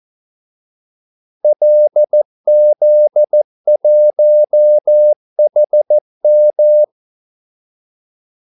Morse code, heard as a keyed tone, about 14 wpm, 600 Hz. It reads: LZ1HM
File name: Callsigns-14wpm